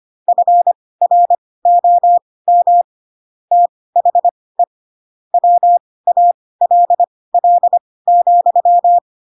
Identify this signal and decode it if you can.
Morse code, heard as a keyed tone, about 25 wpm, spaced as for 18 wpm, 700 Hz. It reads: FROM THE WALL,
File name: Pinocchio / Pinocchio ch4 12 25 ik1hge